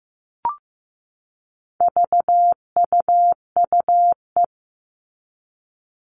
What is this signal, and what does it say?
Morse code, heard as a keyed tone, about 15 wpm, 700 Hz. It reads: VUUE
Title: All Cut Numbers - 4-Digit 15wpm